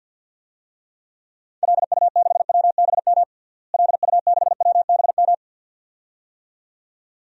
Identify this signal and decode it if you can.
Morse code, heard as a keyed tone, about 53 wpm, 700 Hz. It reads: LU6WBM LU6WBM
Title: DXCC-100 Call Signs - 6 Characters - Repeated 2x - 50wpm